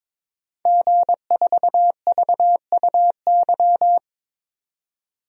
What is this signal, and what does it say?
Morse code, heard as a keyed tone, about 22 wpm, 700 Hz. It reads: G4VUY
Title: DXCC-100 Call Signs - 5 Characters - 22wpm